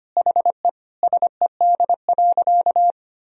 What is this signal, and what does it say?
Morse code, heard as a keyed tone, about 25 wpm, 700 Hz. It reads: HE SED.
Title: Sentences from Top 300 Words and Abbreviations 25wpm